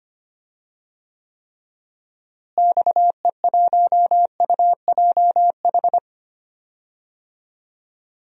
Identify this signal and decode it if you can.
Morse code, heard as a keyed tone, about 25 wpm, 700 Hz. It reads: XE1UJH
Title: DXCC-100 Call Signs - 6 Characters - Repeated 3x - 25wpm